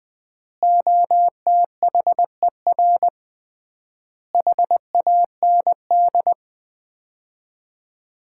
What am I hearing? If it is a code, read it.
Morse code, heard as a keyed tone, about 20 wpm, 700 Hz. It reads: OTHER HAND
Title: 3x Word-Spacing - Sets of 4 Words - 20wpm